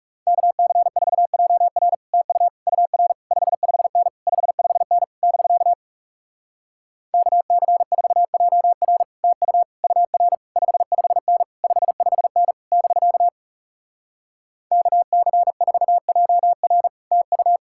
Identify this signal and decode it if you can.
Morse code, heard as a keyed tone, about 40 wpm, 700 Hz. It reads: KC4JR TU UR 55N 55N <BK> KC4JR TU UR 55N 55N <BK> KC4JR TU